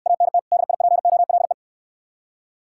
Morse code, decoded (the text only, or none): ANT DIPOLE